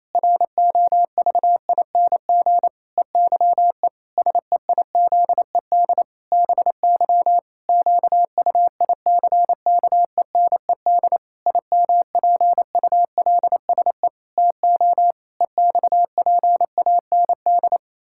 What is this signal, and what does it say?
Morse code, heard as a keyed tone, about 28 wpm, 700 Hz. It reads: ROVING EYE SEIZED BY QUICKENED IMPULSE TO EXPAND
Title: Pangram Challenge v2 - 28wpm